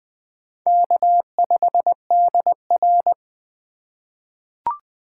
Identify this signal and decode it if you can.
Morse code, heard as a keyed tone, about 20 wpm, 700 Hz. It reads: K5DR
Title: USA Call Signs - 6x Character Spacing - 20wpm